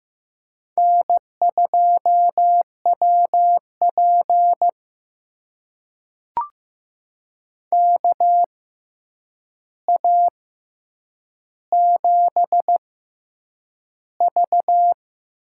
Morse code, heard as a keyed tone, about 15 wpm, spaced as for 8 wpm, 700 Hz. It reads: N2WP KA7V
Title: USA Call Signs - 6x Character Spacing - 15wpm